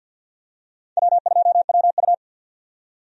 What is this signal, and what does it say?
Morse code, heard as a keyed tone, about 51 wpm, 700 Hz. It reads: W2WU
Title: USA Call Signs - 4 Characters - 50wpm